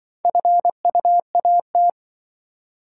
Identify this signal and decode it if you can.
Morse code, heard as a keyed tone, about 23 wpm, 700 Hz. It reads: FUAT
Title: Speed-Racing - Four Letter-Number Combinations - 20wpm